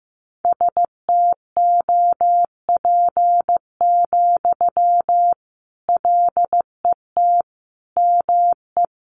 Morse code, heard as a keyed tone, about 15 wpm, 700 Hz. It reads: STOP, LET ME
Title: Sentences from Top 300 Words 15wpm